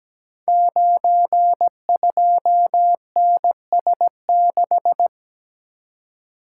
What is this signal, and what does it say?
Morse code, heard as a keyed tone, about 17 wpm, 700 Hz. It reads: 92NS6
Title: Code Groups - 5 Letter:Numbers - 17wpm